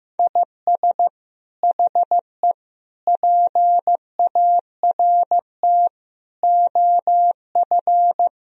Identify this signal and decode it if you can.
Morse code, heard as a keyed tone, about 15 wpm, 700 Hz. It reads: IS HE PART OF
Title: Sentences from Top 1000 Words - Review Part 4 - 15wpm